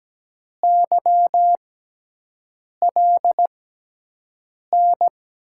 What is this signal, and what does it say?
Morse code, heard as a keyed tone, about 17 wpm, spaced as for 6 wpm, 700 Hz. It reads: YLN
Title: USA Call Signs - 6x Character Spacing - 17wpm